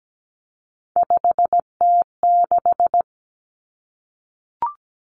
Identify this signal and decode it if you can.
Morse code, heard as a keyed tone, about 17 wpm, 700 Hz. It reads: 5T6
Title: Three Challenging Letter-Number Combinations 17wpm